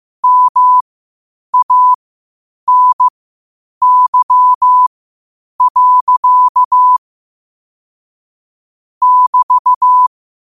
Morse code, heard as a keyed tone, about 15 wpm, spaced as for 8 wpm, 1000 Hz. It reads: MANY. <BT>
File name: qotdfiles / 8wpm